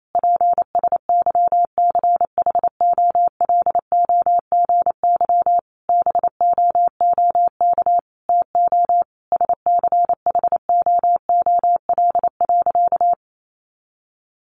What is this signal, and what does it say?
Morse code, heard as a keyed tone, about 28 wpm, 700 Hz. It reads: PSYCHOLOGY BOOK TO SCHOOL.
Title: Sentences from Top 1400 Words 28wpm